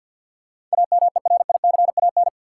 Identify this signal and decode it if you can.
Morse code, heard as a keyed tone, about 48 wpm, 700 Hz. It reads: AMERICAN
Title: Sets of 2 Words - New Word Drill - 50wpm